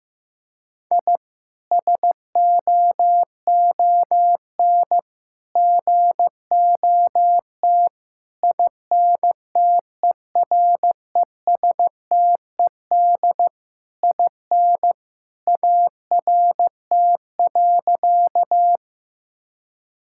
Morse code, heard as a keyed tone, about 15 wpm, 700 Hz. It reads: I SOON GOT INTERESTED IN ART.
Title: Sentences from Top 1000 Words - Review Part 6 - 15wpm